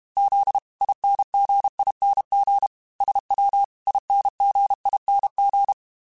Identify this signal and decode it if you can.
Morse code, heard as a keyed tone, about 32 wpm, 800 Hz. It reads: Z INGING SW INGING
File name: Special INGs-mp3s / Special INGs-mp3s / Dble Ings sp @30